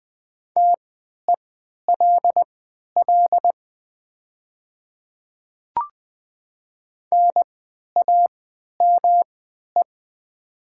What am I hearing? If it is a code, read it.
Morse code, heard as a keyed tone, about 20 wpm, spaced as for 11 wpm, 700 Hz. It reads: TELL NAME
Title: 3x Character Spacing - Top 200 Words - 20wpm